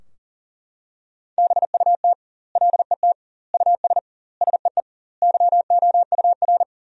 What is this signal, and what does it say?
Morse code, heard as a keyed tone, about 37 wpm, 700 Hz. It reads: BUT LET US SEE YOUR
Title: 2x Word-Spacing - Sentences from Top 500 Words - 40wpm